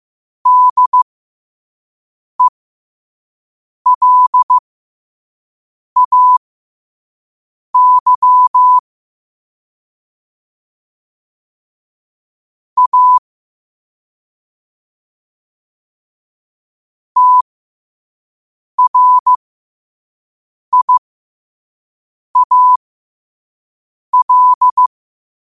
Morse code, heard as a keyed tone, about 15 wpm, spaced as for 5 wpm, 1000 Hz. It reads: DELAY A TRIAL
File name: qotdfiles / 5wpm